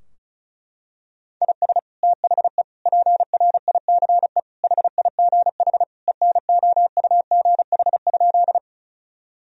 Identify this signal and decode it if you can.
Morse code, heard as a keyed tone, about 34 wpm, 700 Hz. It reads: IS THE PRICE HIGH ENOUGH?
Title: Sentences from Top 1000 Words - Review Part 4 - 35wpm